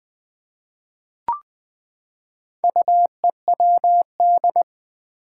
Morse code, heard as a keyed tone, about 20 wpm, 700 Hz. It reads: UEWD
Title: Four Challenging Letter-Number Combinations 20wpm